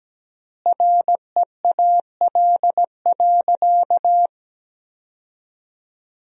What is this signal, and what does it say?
Morse code, heard as a keyed tone, about 17 wpm, 700 Hz. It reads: REAL.
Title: 3x Word-Spacing - Sentences from Top 300 Words - 17wpm